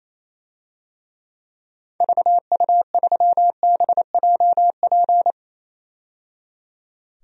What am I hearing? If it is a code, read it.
Morse code, heard as a keyed tone, about 28 wpm, 700 Hz. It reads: VU3BJP
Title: DXCC-100 Call Signs - 6 Characters - 28wpm